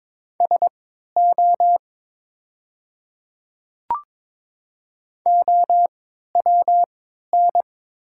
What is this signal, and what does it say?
Morse code, heard as a keyed tone, about 22 wpm, spaced as for 12 wpm, 700 Hz. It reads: SO OWN
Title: 3x Character Spacing - Top 300 Words - 22wpm